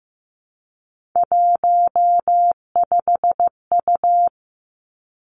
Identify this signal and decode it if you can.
Morse code, heard as a keyed tone, about 15 wpm, 700 Hz. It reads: 15U
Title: Three Challenging Letter-Number Combinations 15wpm